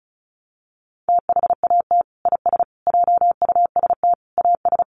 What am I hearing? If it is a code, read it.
Morse code, heard as a keyed tone, about 34 wpm, 700 Hz. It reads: THAT IS JUST AS
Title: Sentences from Top 600 Words 35wpm